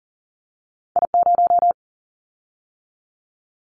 Morse code, heard as a keyed tone, about 42 wpm, 700 Hz. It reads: I0
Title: Callsign Prefixes for Top 50 DXCC - 40wpm